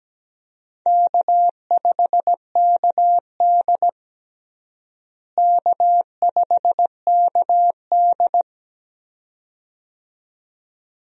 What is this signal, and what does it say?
Morse code, heard as a keyed tone, about 17 wpm, 700 Hz. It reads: K5KD K5KD
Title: USA Call Signs - 4 Characters - Repeated 2x - 17wpm